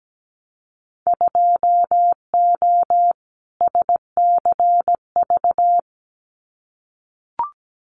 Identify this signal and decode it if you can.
Morse code, heard as a keyed tone, about 17 wpm, 700 Hz. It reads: 2O SCV
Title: Winter Field Day Exchanges Encore 17wpm